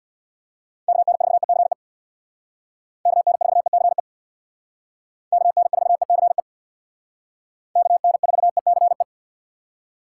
Morse code, heard as a keyed tone, about 61 wpm, 700 Hz. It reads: KN4ECE KN4ECE KN4ECE KN4ECE
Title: Speed Racing - USA 2x3 Call Signs 50wpm